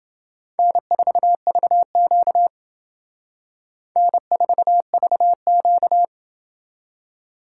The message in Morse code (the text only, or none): N4VQ N4VQ